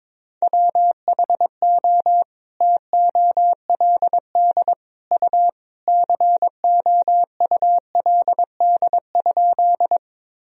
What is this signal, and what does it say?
Morse code, heard as a keyed tone, about 22 wpm, 700 Hz. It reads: WHO TOLD U COULD?
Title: Sentences from Top 400 Words and Abbreviations 22wpm